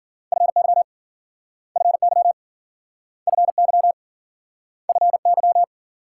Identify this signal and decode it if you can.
Morse code, heard as a keyed tone, about 50 wpm, 700 Hz. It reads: FY FY FY FY